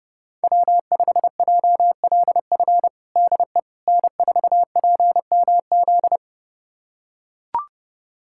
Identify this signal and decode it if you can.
Morse code, heard as a keyed tone, about 30 wpm, 700 Hz. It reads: W5JLF DE N4PMZ
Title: USA Callsign DE Callsign - 5 Characters - 30wpm